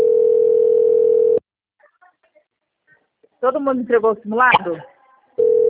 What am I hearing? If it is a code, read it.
Ringback tone (North America), 440 + 480 Hz, with speech.
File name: out- - - - - .34992